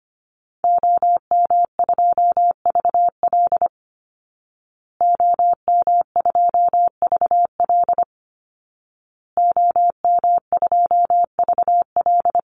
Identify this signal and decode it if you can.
Morse code, heard as a keyed tone, about 25 wpm, 700 Hz. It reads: OM2VL OM2VL OM2VL
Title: Call Signs Repeated 2x - 25wpm